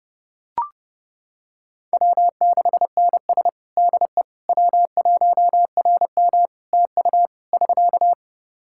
Morse code, heard as a keyed tone, about 30 wpm, 700 Hz. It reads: W6NS DE W1RM TU <SK>
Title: K1USN SST QSO Practice 30wpm